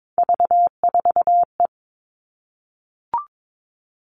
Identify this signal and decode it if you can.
Morse code, heard as a keyed tone, about 22 wpm, 700 Hz. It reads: V4E
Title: Three Challenging Letter-Number Combinations 22wpm